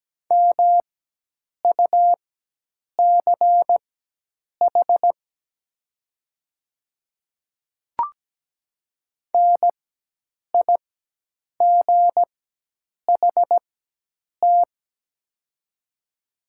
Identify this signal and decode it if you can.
Morse code, heard as a keyed tone, about 17 wpm, spaced as for 8 wpm, 700 Hz. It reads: MUCH NIGHT
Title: 4x Character Spacing - Top 300 Words - 17wpm